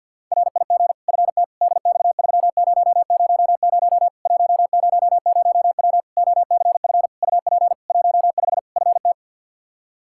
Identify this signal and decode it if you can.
Morse code, heard as a keyed tone, about 50 wpm, 700 Hz. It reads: RIG FT DX3000 100W OCF UP 15 FT
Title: K1USN SST QSO Practice 50wpm